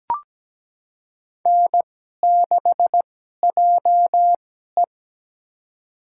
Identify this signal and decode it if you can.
Morse code, heard as a keyed tone, about 17 wpm, spaced as for 12 wpm, 700 Hz. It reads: N6JE
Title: USA Call Signs - 2x Character Spacing - 17wpm